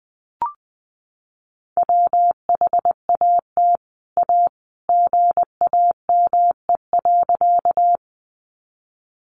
Morse code, heard as a keyed tone, about 20 wpm, 700 Hz. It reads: WHAT A GAME.
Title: Sentences from Top 500 Words 20wpm